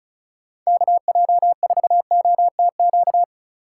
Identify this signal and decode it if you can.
Morse code, heard as a keyed tone, about 36 wpm, 700 Hz. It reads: KJ4OTQ
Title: USA Call Signs - 6 Characters - 2x Character Spacing - 35wpm